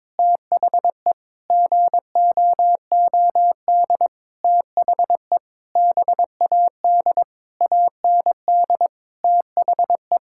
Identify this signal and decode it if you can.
Morse code, heard as a keyed tone, about 22 wpm, 700 Hz. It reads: THE GOOD THE BAD AND THE
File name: Sentences with 7-Word Phrases - 22wpm